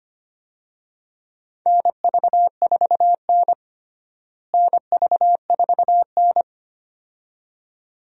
Morse code, heard as a keyed tone, about 25 wpm, 700 Hz. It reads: NV4N NV4N
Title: USA Call Signs - 4 Characters - Repeated 2x - 25wpm